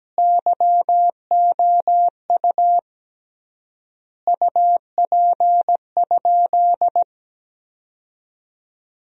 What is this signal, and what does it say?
Morse code, heard as a keyed tone, about 17 wpm, 700 Hz. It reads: YOU UP?
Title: 3x Word-Spacing - Sentences from Top 500 Words - 17wpm